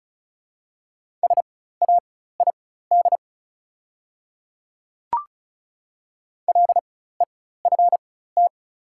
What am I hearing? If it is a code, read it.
Morse code, heard as a keyed tone, about 34 wpm, spaced as for 16 wpm, 700 Hz. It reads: SAID LEFT
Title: 4x Character Spacing - Top 300 Words - 35wpm